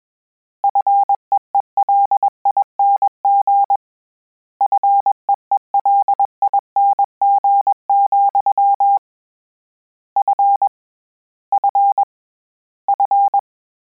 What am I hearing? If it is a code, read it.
Morse code, heard as a keyed tone, about 21 wpm, 800 Hz. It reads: FEELING FEELING, F F F
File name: Combinations / Combinations-mp3s / Combinations@20